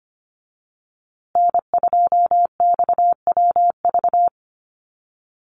Morse code, heard as a keyed tone, about 25 wpm, 700 Hz. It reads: N2XWV
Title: USA Call Signs Repeated 3x 25wpm